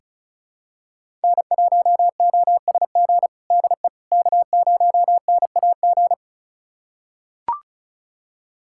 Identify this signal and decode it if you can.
Morse code, heard as a keyed tone, about 34 wpm, 700 Hz. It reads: N1OSG DE K0NAG
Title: USA Callsign DE Callsign - 5 Characters - 35wpm